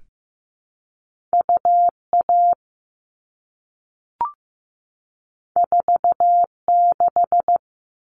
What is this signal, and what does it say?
Morse code, heard as a keyed tone, about 15 wpm, 700 Hz. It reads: UA 46
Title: All Cut Numbers - 2-Digit 15wpm